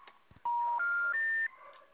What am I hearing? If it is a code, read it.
Special information tone.